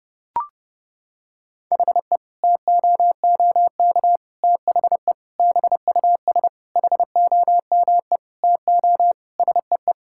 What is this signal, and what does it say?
Morse code, heard as a keyed tone, about 32 wpm, 700 Hz. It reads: HE TOOK THE BUS HOME TO SEE
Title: Sentences from Top 1000 Words - Review Part 6 - 30wpm